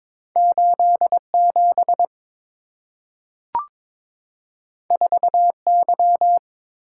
Morse code, heard as a keyed tone, about 22 wpm, 700 Hz. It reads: 87 4Y
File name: practice / 22wpm / Two Letter-Number Combinations - 22wpm